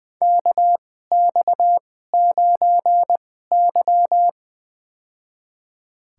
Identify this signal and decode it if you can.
Morse code, heard as a keyed tone, about 20 wpm, spaced as for 14 wpm, 700 Hz. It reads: KX9Y
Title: USA Call Signs - 4 Characters - 2x Character Spacing - 20wpm